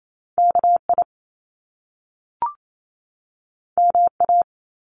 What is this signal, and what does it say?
Morse code, heard as a keyed tone, about 28 wpm, 700 Hz. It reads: KI MA
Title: Top Two Letter Combinations in Words 28wpm